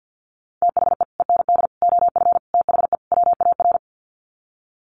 Morse code, heard as a keyed tone, about 48 wpm, 700 Hz. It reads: THE END OF THE WAR
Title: Sets of 5 Words 50wpm